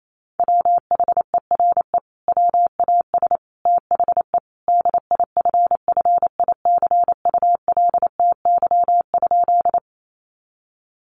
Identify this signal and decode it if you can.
Morse code, heard as a keyed tone, about 28 wpm, 700 Hz. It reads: WHERE WAS THE DIFFICULTY?
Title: Sentences from Top 1500 Words 28wpm